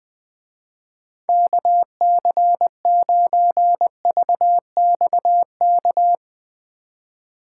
Morse code, heard as a keyed tone, about 20 wpm, 700 Hz. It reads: KC9VXK